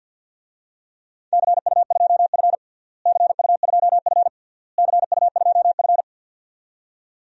Morse code, heard as a keyed tone, about 50 wpm, 700 Hz. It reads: CU2F CU2F CU2F
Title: DXCC-100 Call Signs - 4 Characters - Repeated 3x - 50wpm